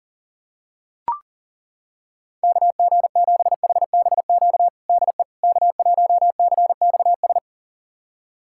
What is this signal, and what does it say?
Morse code, heard as a keyed tone, about 38 wpm, 700 Hz. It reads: KG7HBQ DE K1CXS
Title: USA Callsign DE Callsign 40wpm